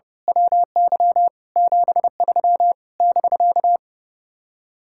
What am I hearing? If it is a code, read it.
Morse code, heard as a keyed tone, about 28 wpm, 700 Hz. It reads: WY 73 <BK>
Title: Speed-Racing Common QSO Phrases - 30wpm